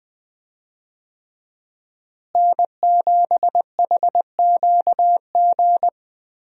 Morse code, heard as a keyed tone, about 20 wpm, 700 Hz. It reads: N7HQG
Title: USA Call Signs - 5 Characters - Repeated 3x - 20wpm